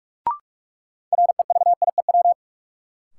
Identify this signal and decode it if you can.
Morse code, heard as a keyed tone, about 46 wpm, 700 Hz. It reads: REVIEW
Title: Sentences from Top 900 Words - New Word Drill - 45wpm